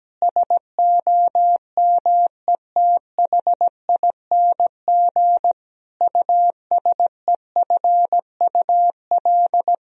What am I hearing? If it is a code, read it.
Morse code, heard as a keyed tone, about 17 wpm, 700 Hz. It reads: SOMETHING USEFUL
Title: Sentences from Top 1000 Words - Review Part 4 - 17wpm